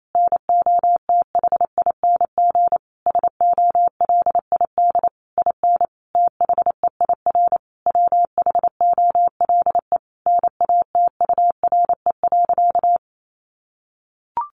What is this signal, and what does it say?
Morse code, heard as a keyed tone, about 28 wpm, 700 Hz. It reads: NOTHING SOLID IN THEIR WHOLE NATURE.
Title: Sentences from Top 1500 Words 28wpm